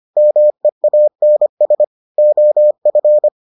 Morse code, heard as a keyed tone, about 25 wpm, 600 Hz. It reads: MEANS OF